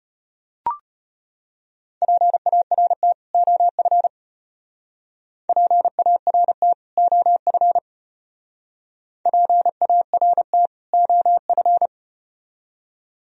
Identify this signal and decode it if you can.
Morse code, heard as a keyed tone, about 34 wpm, 700 Hz. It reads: PART OF PART OF PART OF